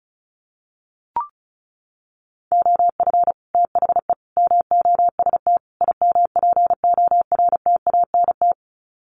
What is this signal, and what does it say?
Morse code, heard as a keyed tone, about 34 wpm, 700 Hz. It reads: OF THE MOST IMPORTANT
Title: Sets of 4 Words 35wpm